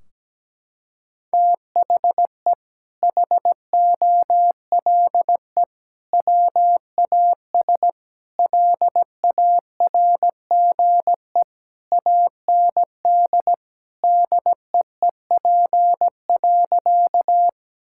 Morse code, heard as a keyed tone, about 17 wpm, 700 Hz. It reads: THE HOLE WAS LARGE AND DEEP.
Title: Sentences from Top 1600 Words 17wpm